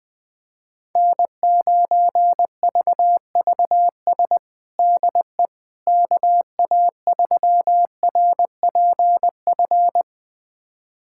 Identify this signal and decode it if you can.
Morse code, heard as a keyed tone, about 20 wpm, 700 Hz. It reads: N9VVS DE KA3RPF
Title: USA Callsign DE Callsign 20wpm